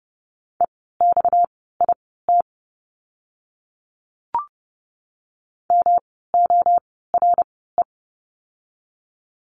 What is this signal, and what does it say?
Morse code, heard as a keyed tone, about 30 wpm, spaced as for 17 wpm, 700 Hz. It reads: EXIT MORE
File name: course / 30:10wpm / 230 Lesson 37f - Letter x - Words 30wpm